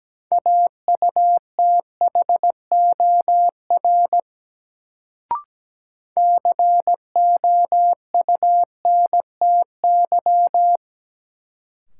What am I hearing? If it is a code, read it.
Morse code, heard as a keyed tone, about 17 wpm, 700 Hz. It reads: AUTHOR COUNTY
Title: Sentences from Top 1200 Words - New Word Drill Alt - 17wpm